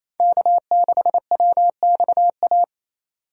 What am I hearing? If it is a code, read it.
Morse code, heard as a keyed tone, about 35 wpm, 700 Hz. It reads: K6WXA